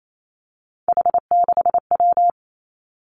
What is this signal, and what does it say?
Morse code, heard as a keyed tone, about 28 wpm, 690 Hz. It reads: H6W
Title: Three Challenging Letter-Number Combinations 28wpm